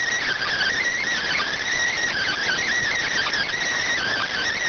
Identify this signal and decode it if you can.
Emergency Alert System (SAME) tones, heard as harsh buzzing bursts. It is heard from 0.4 to 4.4 s.